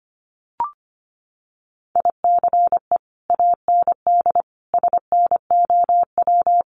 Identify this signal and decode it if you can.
Morse code, heard as a keyed tone, about 25 wpm, 700 Hz. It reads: ICE AND SNOW
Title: Sentences from Top 1000 Words 25wpm